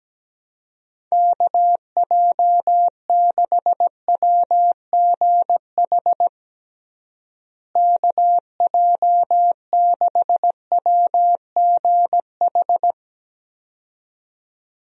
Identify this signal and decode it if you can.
Morse code, heard as a keyed tone, about 17 wpm, 700 Hz. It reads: KJ6WGH KJ6WGH
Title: USA Call Signs - 6 Characters - Repeated 2x - 17wpm